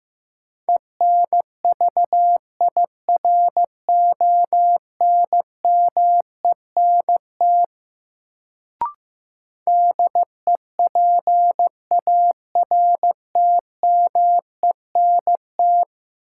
Morse code, heard as a keyed tone, about 15 wpm, 700 Hz. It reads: ENVIRONMENT DEPARTMENT
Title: Sentences from Top 1000 Words - Review Part 5 - New Word Drill - 15wpm